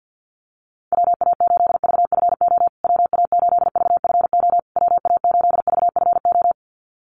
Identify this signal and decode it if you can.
Morse code, heard as a keyed tone, about 48 wpm, 700 Hz. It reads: WA8VFO WA8VFO WA8VFO
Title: USA Call Signs Repeated 3x 50wpm